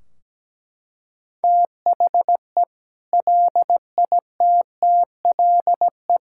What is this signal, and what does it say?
Morse code, heard as a keyed tone, about 17 wpm, 700 Hz. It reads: THE LITTLE
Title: Sentences from Top 1600 Words 17wpm